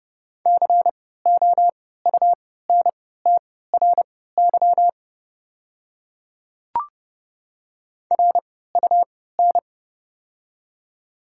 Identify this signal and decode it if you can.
Morse code, heard as a keyed tone, about 30 wpm, spaced as for 17 wpm, 700 Hz. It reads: COUNTRY RUN
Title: 3x Character Spacing - Top 300 Words - 30wpm